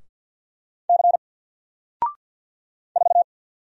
Morse code, heard as a keyed tone, about 50 wpm, 700 Hz. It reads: C 4
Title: Single Letter-Number Alt 50wpm